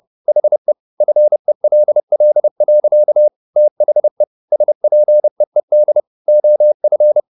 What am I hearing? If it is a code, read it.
Morse code, heard as a keyed tone, about 30 wpm, 600 Hz. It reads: HE FELL. THE SPEED OF